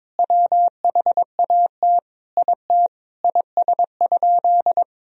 Morse code, heard as a keyed tone, about 21 wpm, 700 Hz. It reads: WHAT IT IS?
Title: Sentences from Top 100 Words and Abbreviations 22wpm